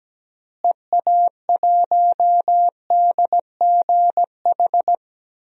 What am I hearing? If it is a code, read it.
Morse code, heard as a keyed tone, about 17 wpm, 700 Hz. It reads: EA1DGH
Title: DXCC-100 Call Signs - 6 Characters - Repeated 3x - 17wpm